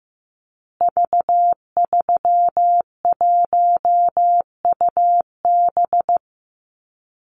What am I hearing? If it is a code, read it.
Morse code, heard as a keyed tone, about 15 wpm, 700 Hz. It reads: V31UB
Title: Call Signs Repeated 2x - 15wpm